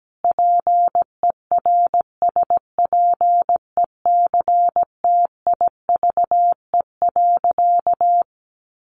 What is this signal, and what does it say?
Morse code, heard as a keyed tone, about 17 wpm, 700 Hz. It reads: PERSPECTIVE.
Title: Sentences from Top 1200 Words 17wpm